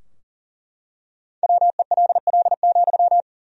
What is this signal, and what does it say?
Morse code, heard as a keyed tone, about 40 wpm, 700 Hz. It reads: WELL,
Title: 2x Word-Spacing - Sentences from Top 200 Words - 40wpm